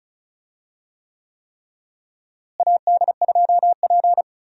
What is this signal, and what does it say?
Morse code, heard as a keyed tone, about 36 wpm, 700 Hz. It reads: AD2P
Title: USA Call Signs - 4 Characters - Repeated 3x - 35wpm